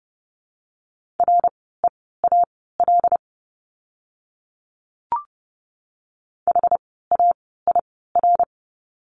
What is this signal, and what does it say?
Morse code, heard as a keyed tone, about 30 wpm, spaced as for 17 wpm, 700 Hz. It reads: REAL HAIR
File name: Lesson 17f - Letter u - Words 30wpm